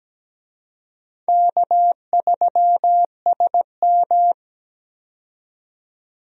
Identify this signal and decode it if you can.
Morse code, heard as a keyed tone, about 17 wpm, 700 Hz. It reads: K3SM
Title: USA Call Signs - 4 Characters - 17wpm